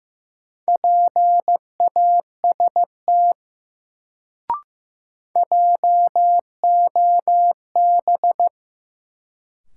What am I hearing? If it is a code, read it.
Morse code, heard as a keyed tone, about 15 wpm, 700 Hz. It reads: PAST JOB
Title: Sentences from Top 600 Words - New Word Drill - 15wpm